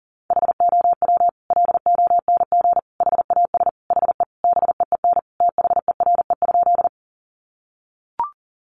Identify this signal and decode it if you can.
Morse code, heard as a keyed tone, about 40 wpm, 700 Hz. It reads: HOW LONG HAS HE BEEN THERE?
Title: Sentences from Top 100 Words 40wpm